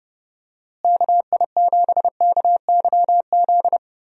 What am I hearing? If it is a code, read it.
Morse code, heard as a keyed tone, about 30 wpm, 700 Hz. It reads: KI7KYZ
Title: USA Call Signs - 6 Characters - Repeated 2x - 30wpm